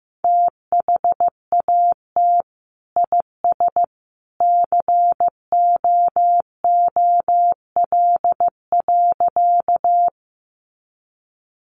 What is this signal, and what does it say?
Morse code, heard as a keyed tone, about 15 wpm, 700 Hz. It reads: THAT IS COOL.
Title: Sentences from Top 1200 Words 15wpm